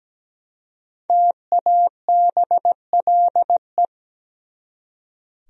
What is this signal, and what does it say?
Morse code, heard as a keyed tone, about 20 wpm, 700 Hz. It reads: TABLE
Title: Speed-Racing Top 500 Words 17wpm